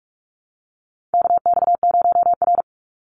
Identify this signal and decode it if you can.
Morse code, heard as a keyed tone, about 44 wpm, 700 Hz. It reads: KX0R
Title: NA SOTA Activator Callsigns 45wpm